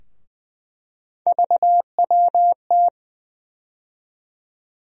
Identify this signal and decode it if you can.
Morse code, heard as a keyed tone, about 20 wpm, 700 Hz. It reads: VWT
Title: USA Call Sign Postfixes - 3 Letters 20wpm